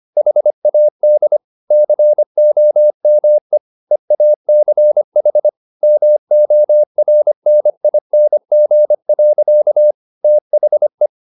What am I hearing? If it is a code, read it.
Morse code, heard as a keyed tone, about 25 wpm, 600 Hz. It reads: HAD COME EACH MORNING. THE